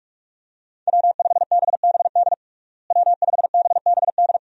Morse code, heard as a keyed tone, about 46 wpm, 700 Hz. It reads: W5BBD W5BBD
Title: USA Call Signs - 5 Characters - Repeated 2x - 45wpm